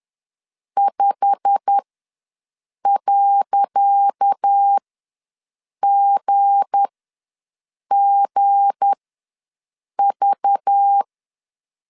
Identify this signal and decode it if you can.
Morse code, heard as a keyed tone, about 10 wpm, spaced as for 6 wpm, 800 Hz. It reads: 5.GGV
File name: LEZIONI / 10 SIGNS / 10C DOT ws 10wpm 4213